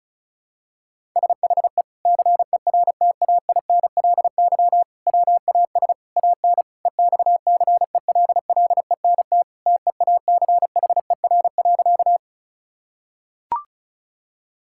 Morse code, heard as a keyed tone, about 34 wpm, 700 Hz. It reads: SHE CERTAINLY WAS AN EXCELLENT TEACHER.
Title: Sentences from Top 1000 Words - Review Part 6 - 35wpm